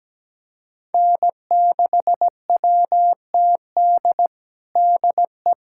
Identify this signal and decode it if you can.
Morse code, heard as a keyed tone, about 17 wpm, 700 Hz. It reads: N6WTD DE
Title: USA Callsign DE Callsign - 5 Characters - 17wpm